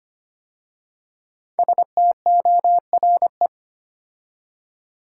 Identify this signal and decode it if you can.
Morse code, heard as a keyed tone, about 25 wpm, 700 Hz. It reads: STORE
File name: 5 Letter Words - First 3 Letters Given - 25wpm